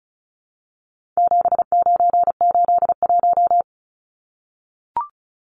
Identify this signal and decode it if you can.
Morse code, heard as a keyed tone, about 34 wpm, 700 Hz. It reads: 7981
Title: Numbers - 4-Digit 35wpm